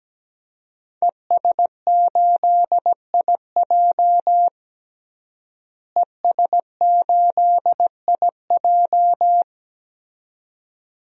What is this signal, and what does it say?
Morse code, heard as a keyed tone, about 17 wpm, 700 Hz. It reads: ES8IJ ES8IJ
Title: DXCC-100 Call Signs - 5 Characters - Repeated 2x - 17wpm